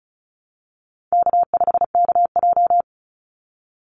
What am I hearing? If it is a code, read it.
Morse code, heard as a keyed tone, about 34 wpm, 700 Hz. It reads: K5KJ
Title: Call Signs Repeated 2x - 35wpm